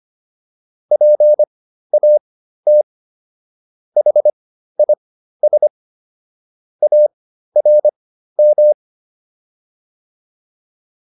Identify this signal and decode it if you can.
Morse code, heard as a keyed tone, about 25 wpm, spaced as for 13 wpm, 600 Hz. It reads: PAT HIS ARM